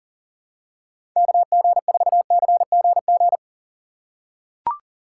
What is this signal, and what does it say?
Morse code, heard as a keyed tone, about 40 wpm, 700 Hz. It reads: KG4CGG
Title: USA Call Signs - 6 Characters - 2x Character Spacing - 40wpm